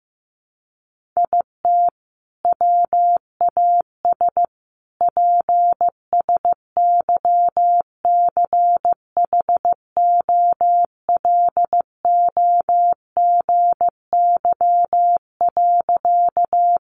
Morse code, heard as a keyed tone, about 15 wpm, 700 Hz. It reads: IT WAS PSYCHOLOGY.
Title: Sentences from Top 1400 Words 15wpm